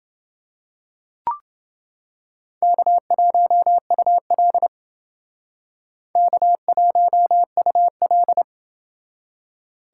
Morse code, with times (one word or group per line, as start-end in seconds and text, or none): 2.6-4.7 K1UL
6.2-8.4 K1UL